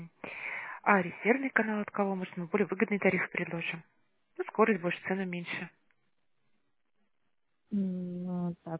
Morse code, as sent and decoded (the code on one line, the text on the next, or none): none